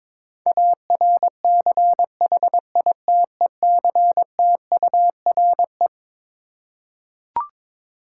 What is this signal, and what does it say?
Morse code, heard as a keyed tone, about 22 wpm, 700 Hz. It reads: ARCHITECTURE
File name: Twelve Letter Words - 22wpm